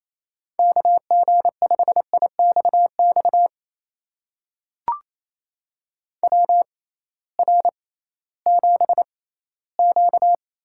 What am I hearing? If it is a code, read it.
Morse code, heard as a keyed tone, about 28 wpm, spaced as for 16 wpm, 700 Hz. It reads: KG5IXX WR7Q
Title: USA Call Signs - 6x Character Spacing - 28wpm